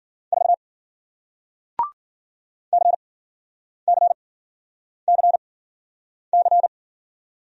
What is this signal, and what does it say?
Morse code, heard as a keyed tone, about 53 wpm, spaced as for 9 wpm, 700 Hz. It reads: 4 CCCC